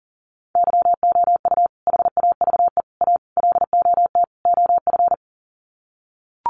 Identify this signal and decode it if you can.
Morse code, heard as a keyed tone, about 40 wpm, 700 Hz. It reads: YOU HAVE A LOT OF
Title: Sets of 5 Words Encore 40wpm